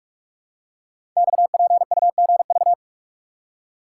Morse code, heard as a keyed tone, about 44 wpm, 700 Hz. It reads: XPUGV
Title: Code Groups - 5 Letters - 45wpm